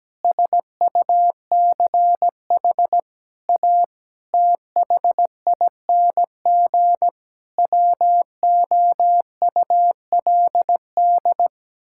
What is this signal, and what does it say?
Morse code, heard as a keyed tone, about 17 wpm, 700 Hz. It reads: SUCH A THING WOULD
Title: Sentences from Top 1000 Words - Review Part 6 - 17wpm